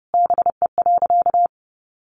Morse code, heard as a keyed tone, about 30 wpm, 700 Hz. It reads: BE.
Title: Sentences from Top 1100 Words 30wpm